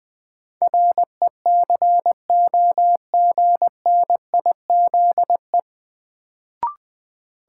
Morse code, heard as a keyed tone, about 20 wpm, 700 Hz. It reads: RECOGNIZE
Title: Sentences from Top 1400 Words - New Word Drill Alt - 20wpm